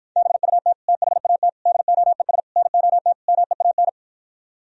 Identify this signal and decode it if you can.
Morse code, heard as a keyed tone, about 52 wpm, 700 Hz. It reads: BUT THAT DOES NOT MEAN